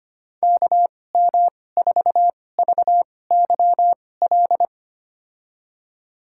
Morse code, heard as a keyed tone, about 25 wpm, spaced as for 18 wpm, 700 Hz. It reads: KM4VYL
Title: USA Call Signs - 6 Characters - 2x Character Spacing - 25wpm